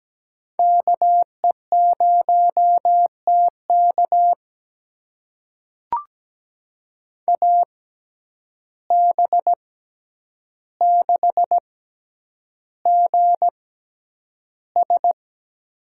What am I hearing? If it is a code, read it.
Morse code, heard as a keyed tone, about 17 wpm, spaced as for 9 wpm, 700 Hz. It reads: KE0TK AB6GS
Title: USA Call Signs - 5 Characters - 6x Character Spacing - 17wpm